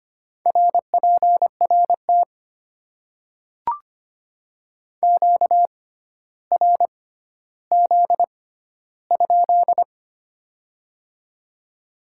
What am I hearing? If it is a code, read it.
Morse code, heard as a keyed tone, about 25 wpm, spaced as for 13 wpm, 700 Hz. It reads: RPRT QRZ?
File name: Top 90 QSO Elements - 6x Character Spacing - 25wpm